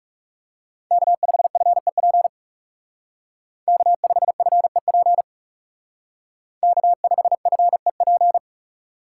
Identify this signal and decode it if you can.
Morse code, heard as a keyed tone, about 37 wpm, 700 Hz. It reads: K5FEP K5FEP K5FEP
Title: Speed Racing - USA 1x3 Call Signs 30wpm